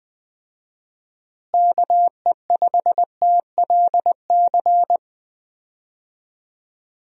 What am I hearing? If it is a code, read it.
Morse code, heard as a keyed tone, about 20 wpm, 700 Hz. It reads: KE5TLC
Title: USA Call Signs - 6 Characters - Repeated 3x - 20wpm